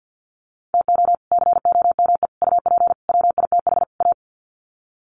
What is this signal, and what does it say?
Morse code, heard as a keyed tone, about 50 wpm, 700 Hz. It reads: TO COME UP WITH A
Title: Sets of 5 Words 50wpm